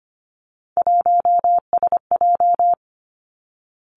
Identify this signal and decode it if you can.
Morse code, heard as a keyed tone, about 25 wpm, 700 Hz. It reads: 1SJ
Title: Three Challenging Letter-Number Combinations 25wpm